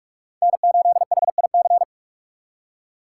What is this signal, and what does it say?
Morse code, heard as a keyed tone, about 41 wpm, 700 Hz. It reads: N8HIC
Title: USA Call Signs - 5 Characters - 4x Character Spacing - 45wpm